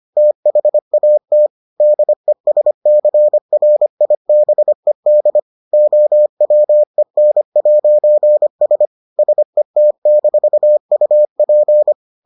Morse code, heard as a keyed tone, about 25 wpm, 600 Hz. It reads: THAT DESCRIBED OWEN'S SET-UP